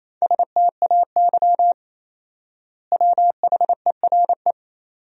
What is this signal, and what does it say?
Morse code, heard as a keyed tone, about 28 wpm, 700 Hz. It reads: STAY WHERE
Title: 4x Word-Spacing - Sentences from Top 500 Words - 28wpm